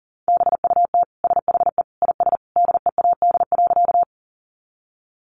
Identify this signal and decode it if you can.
Morse code, heard as a keyed tone, about 40 wpm, 700 Hz. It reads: BUT SHE IS DEAD.
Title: Sentences from Top 1500 Words 40wpm